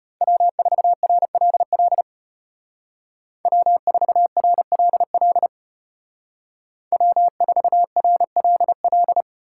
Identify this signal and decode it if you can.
Morse code, heard as a keyed tone, about 31 wpm, 700 Hz. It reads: W4RLL W4RLL W4RLL
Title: Speed Racing - USA 1x3 Call Signs 25wpm